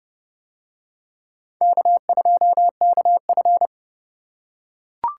Morse code, heard as a keyed tone, about 29 wpm, 700 Hz. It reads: K2KF